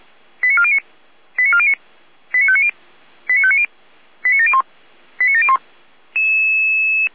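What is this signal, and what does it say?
Five-tone selective call (ZVEI), heard as quick runs of five tones, tones of about 70 ms. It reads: 89409 89409 89590 89590 89701 89701
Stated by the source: ZVEI five-tone selective call 89126 received by SDR.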